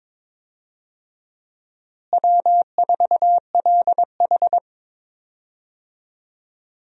Morse code, heard as a keyed tone, about 22 wpm, 700 Hz. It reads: W4LH
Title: USA Call Signs - 4 Characters - Repeated 2x - 22wpm